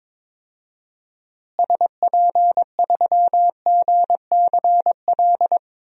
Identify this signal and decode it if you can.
Morse code, heard as a keyed tone, about 21 wpm, 700 Hz. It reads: SP3GCL
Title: DXCC-100 Call Signs - 6 Characters - Repeated 2x - 22wpm